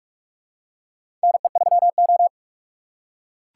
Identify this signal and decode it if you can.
Morse code, heard as a keyed tone, about 46 wpm, 700 Hz. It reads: NE3O
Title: USA Call Signs - 4x Character Spacing - 45wpm